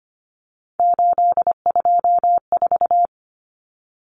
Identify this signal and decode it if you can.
Morse code, heard as a keyed tone, about 25 wpm, 700 Hz. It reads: 824
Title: Common Cut Numbers - 3-Digit 25wpm